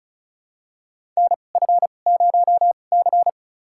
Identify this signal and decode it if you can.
Morse code, heard as a keyed tone, about 36 wpm, spaced as for 25 wpm, 690 Hz. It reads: NF0C
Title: USA Call Signs - 4 Characters - 2x Character Spacing - 35wpm